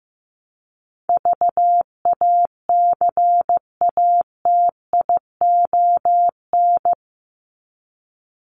Morse code, heard as a keyed tone, about 15 wpm, 700 Hz. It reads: VACATION
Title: Top 1501-1750 Words 15wpm